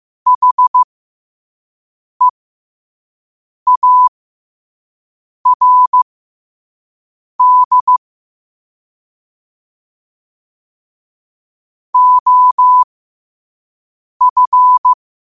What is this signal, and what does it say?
Morse code, heard as a keyed tone, about 15 wpm, spaced as for 5 wpm, 1000 Hz. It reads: HEARD OF
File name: qotdfiles / 5wpm